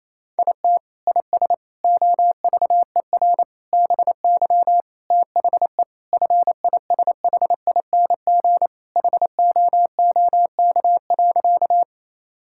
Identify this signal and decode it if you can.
Morse code, heard as a keyed tone, about 28 wpm, 700 Hz. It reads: IT IS OVER BY THE FISHING HOOK.
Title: Sentences from Top 1600 Words 28wpm